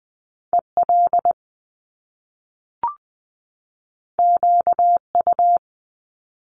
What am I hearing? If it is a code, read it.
Morse code, heard as a keyed tone, about 20 wpm, 700 Hz. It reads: EL QU
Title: Top Two Letter Combinations in Words 20wpm